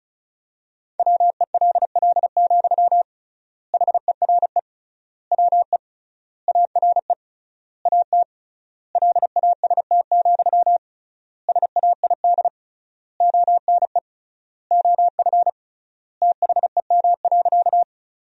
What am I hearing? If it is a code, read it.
Morse code, heard as a keyed tone, about 36 wpm, 700 Hz. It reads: WELL, HERE WE ARE AT LAST, SAID ONE OF THEM.
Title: 3x Word-Spacing - Sentences from Top 300 Words - 35wpm